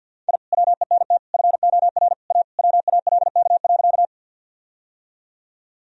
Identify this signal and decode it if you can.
Morse code, heard as a keyed tone, about 51 wpm, 700 Hz. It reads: I WENT FOR A WALK.
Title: Sentences from Top 1000 Words - Review Part 1 - 50wpm